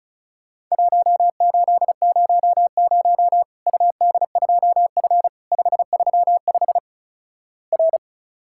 Morse code, heard as a keyed tone, about 36 wpm, 700 Hz. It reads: 1800 UD2F 535 R
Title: Worked All Europe DX Contest - QTC Exchange - 35wpm